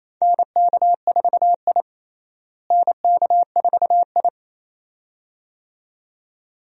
Morse code, heard as a keyed tone, about 28 wpm, 700 Hz. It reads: NK4I NK4I
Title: USA Call Signs - 4 Characters - Repeated 2x - 28wpm